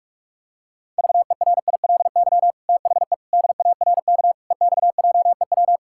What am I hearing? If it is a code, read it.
Morse code, heard as a keyed tone, about 46 wpm, 700 Hz. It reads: VERILY THE DARK EX JEW
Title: Pangram Challenge v2 - 45wpm